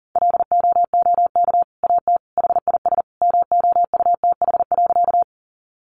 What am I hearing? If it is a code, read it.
Morse code, heard as a keyed tone, about 40 wpm, 700 Hz. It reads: LOOK AT HIS MOUTH.
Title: Sentences from Top 1500 Words 40wpm